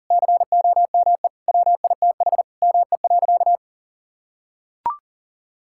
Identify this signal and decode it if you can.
Morse code, heard as a keyed tone, about 40 wpm, 700 Hz. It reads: COME WITH ME.
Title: Sentences from Top 1000 Words - Review Part 6 - 40wpm